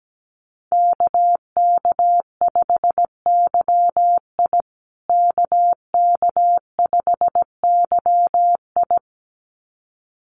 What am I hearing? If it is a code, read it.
Morse code, heard as a keyed tone, about 17 wpm, 700 Hz. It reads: KK5YI KK5YI
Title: USA Call Signs Repeated 2x 17wpm